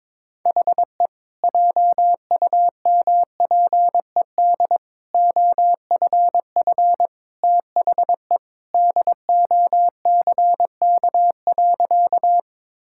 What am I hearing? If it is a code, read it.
Morse code, heard as a keyed tone, about 22 wpm, 700 Hz. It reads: HE JUMPED OFF THE DOCK.
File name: practice / 22wpm / Sentences with 2-Word Phrases - 22wpm